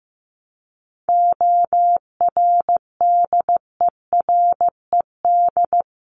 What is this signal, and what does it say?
Morse code, heard as a keyed tone, about 15 wpm, 700 Hz. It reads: ORDERED
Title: A Sound Of Thunder 15wpm